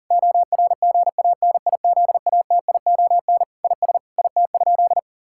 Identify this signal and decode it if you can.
Morse code, heard as a keyed tone, about 37 wpm, 700 Hz. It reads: ORGANIZATION IS IT?
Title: Sentences from Top 1000 Words - Review Part 5 - 40wpm